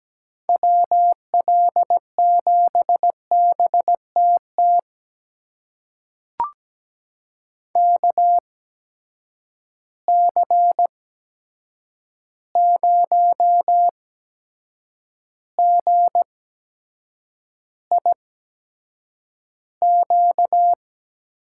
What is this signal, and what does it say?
Morse code, heard as a keyed tone, about 17 wpm, spaced as for 7 wpm, 700 Hz. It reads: WL7BTT KC0GIQ
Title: USA Call Signs - 6 Characters - 8x Character Spacing - 17wpm